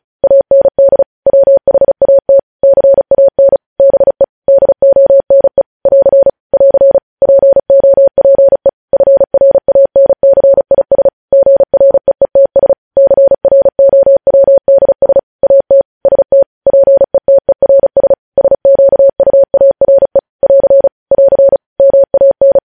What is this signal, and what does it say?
Morse code, heard as a keyed tone, about 36 wpm, 550 Hz. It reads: AND WHAT CAN BE DONE <AR> <AR> POPE FRANCIS GREETS CROWDS AT ST PETERS SQUARE <AR> <AR> MAN
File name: podcast / News headlines-1x-35x35